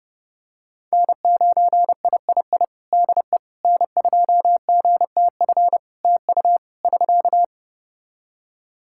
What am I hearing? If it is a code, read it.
Morse code, heard as a keyed tone, about 28 wpm, 700 Hz. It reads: N9III DE N2GTF TU <SK>
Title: Practice QSO - Exchange 4 - Ending - 30wpm